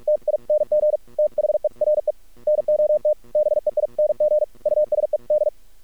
Morse code, heard as a keyed tone, about 49 wpm, 610 Hz. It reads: TING THERE NOT BEING USED
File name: archived / CRSnet